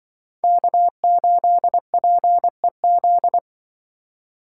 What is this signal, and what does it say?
Morse code, heard as a keyed tone, about 25 wpm, 700 Hz. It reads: K8PEZ
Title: Speed Racing - USA 1x3 Call Signs 20wpm